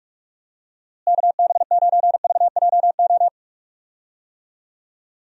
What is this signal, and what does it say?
Morse code, heard as a keyed tone, about 44 wpm, 700 Hz. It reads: KB9VJO
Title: USA Call Signs - 6 Characters - Repeated 3x - 45wpm